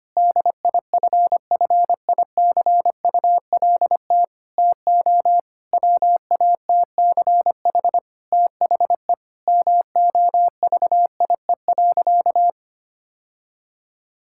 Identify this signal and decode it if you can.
Morse code, heard as a keyed tone, about 25 wpm, 700 Hz. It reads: DIFFICULT TO WATCH THE MOVIE.
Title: Sentences from Top 1000 Words - Review Part 5 - 25wpm